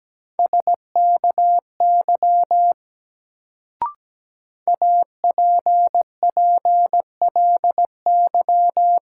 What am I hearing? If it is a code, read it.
Morse code, heard as a keyed tone, about 17 wpm, 700 Hz. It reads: SKY APPLY
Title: Sentences from Top 1200 Words - New Word Drill Alt - 17wpm